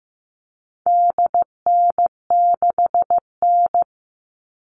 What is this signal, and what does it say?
Morse code, heard as a keyed tone, about 15 wpm, 700 Hz. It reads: DN6N
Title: All Cut Numbers - 4-Digit 15wpm